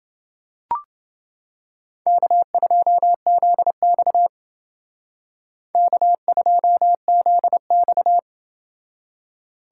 Morse code, heard as a keyed tone, about 30 wpm, 700 Hz. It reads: K2ZX K2ZX